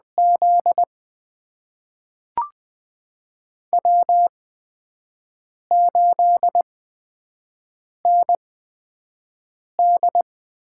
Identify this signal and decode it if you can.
Morse code, heard as a keyed tone, about 20 wpm, spaced as for 5 wpm, 700 Hz. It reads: Z W8ND
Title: USA Call Signs - 4 Characters - 8x Character Spacing - 20wpm